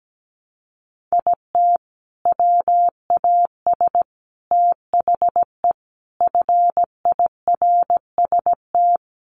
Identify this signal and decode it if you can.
Morse code, heard as a keyed tone, about 17 wpm, 700 Hz. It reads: IT WAS THE FIRST
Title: Sentences from Top 300 Words 17wpm